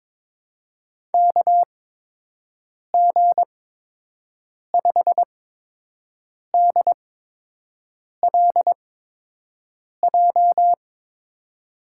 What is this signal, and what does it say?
Morse code, heard as a keyed tone, about 22 wpm, spaced as for 6 wpm, 700 Hz. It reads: KG5DLJ
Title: USA Call Signs - 6 Characters - 8x Character Spacing - 22wpm